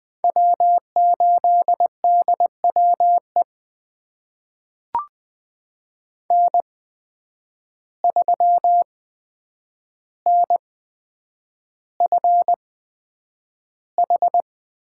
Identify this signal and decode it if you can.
Morse code, heard as a keyed tone, about 20 wpm, spaced as for 9 wpm, 700 Hz. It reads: W8DWE N3NFH
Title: USA Call Signs - 5 Characters - 8x Character Spacing - 20wpm